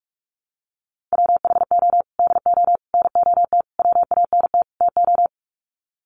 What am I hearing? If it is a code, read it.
Morse code, heard as a keyed tone, about 44 wpm, 700 Hz. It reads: WHO DO NOT WANT TO
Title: Sets of 5 Words Encore2 45wpm